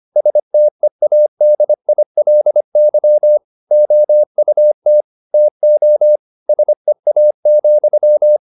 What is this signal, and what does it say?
Morse code, heard as a keyed tone, about 25 wpm, 600 Hz. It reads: STEADILY OUT TO SEA,
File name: the old man and the sea part 1 25wpm